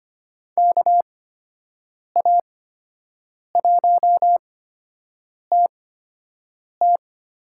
Morse code, heard as a keyed tone, about 25 wpm, spaced as for 7 wpm, 700 Hz. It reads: KA1TT
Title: USA Call Signs - 6 Characters - 8x Character Spacing - 25wpm